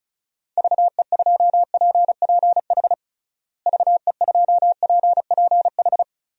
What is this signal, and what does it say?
Morse code, heard as a keyed tone, about 36 wpm, 700 Hz. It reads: VE2PPH VE2PPH
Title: DXCC-100 Call Signs - 6 Characters - Repeated 3x - 35wpm